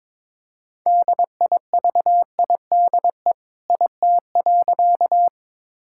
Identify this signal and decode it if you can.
Morse code, heard as a keyed tone, about 22 wpm, 700 Hz. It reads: DIVIDE IT.
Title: Sentences from Top 1600 Words 22wpm